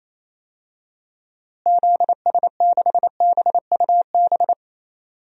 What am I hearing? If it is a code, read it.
Morse code, heard as a keyed tone, about 28 wpm, 700 Hz. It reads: ZS6BUB
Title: DXCC-100 Call Signs - 6 Characters - Repeated 2x - 28wpm